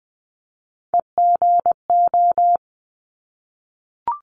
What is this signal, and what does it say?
Morse code, heard as a keyed tone, about 20 wpm, 700 Hz. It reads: EGO
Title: 260 Lesson 42e - Words 20-20wpm